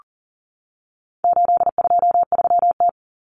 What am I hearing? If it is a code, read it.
Morse code, heard as a keyed tone, about 40 wpm, 700 Hz. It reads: 823T
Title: Common Cut Numbers - 4-Digit 40wpm